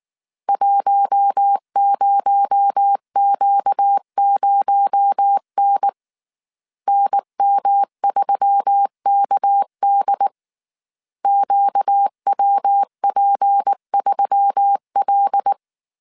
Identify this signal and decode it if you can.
Morse code, heard as a keyed tone, about 19 wpm, 800 Hz. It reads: 10Q0N NM3KD QWP3L
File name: CW / DET / 1 6 1500C S A 18wpm 2255